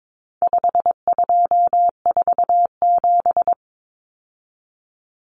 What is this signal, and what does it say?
Morse code, heard as a keyed tone, about 22 wpm, 700 Hz. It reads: 5247
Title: Common Cut Numbers - 4-Digit 22wpm